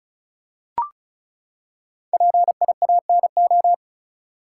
Morse code, heard as a keyed tone, about 34 wpm, 700 Hz. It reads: PIANO
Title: Five Letter Words - 35wpm